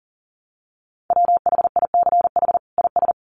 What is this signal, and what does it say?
Morse code, heard as a keyed tone, about 40 wpm, 700 Hz. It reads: WHICH IS
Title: Sets of 2 Words 40wpm